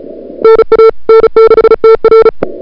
Morse code, heard as a keyed tone, about 38 wpm, 440 Hz. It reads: NA N6TR
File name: CWSprint / February 2007 CW Sprint / 01-2007 CW Sprint Hour1